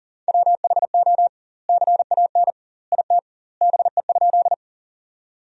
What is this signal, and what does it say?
Morse code, heard as a keyed tone, about 40 wpm, 700 Hz. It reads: WHO CAN IT BE?
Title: 2x Word-Spacing - Sentences from Top 100 Words - 40wpm